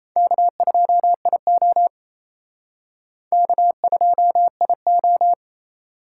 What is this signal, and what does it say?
Morse code, heard as a keyed tone, about 36 wpm, 700 Hz. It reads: K2IO K2IO